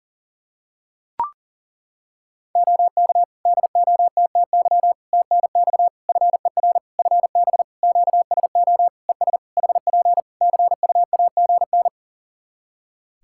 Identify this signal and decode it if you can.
Morse code, heard as a keyed tone, about 40 wpm, 700 Hz. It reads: OK DOTTY TNX FER FB QSO ES HP CUAGN
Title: Practice QSO - Exchange 4 - Ending - 40wpm